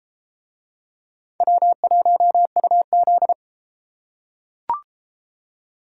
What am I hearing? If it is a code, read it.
Morse code, heard as a keyed tone, about 31 wpm, 700 Hz. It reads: W1UZ